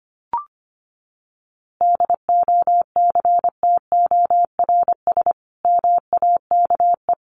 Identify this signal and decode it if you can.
Morse code, heard as a keyed tone, about 25 wpm, 700 Hz. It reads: DOCTORS MAKE
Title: Easy Phrases 25wpm